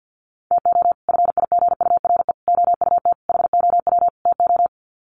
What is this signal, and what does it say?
Morse code, heard as a keyed tone, about 48 wpm, 700 Hz. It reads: TO FIGURE OUT HOW TO
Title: Sets of 5 Words 50wpm